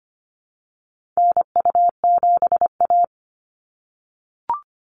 Morse code, heard as a keyed tone, about 25 wpm, 700 Hz. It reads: NU7A
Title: NA SOTA Activator Callsigns 25wpm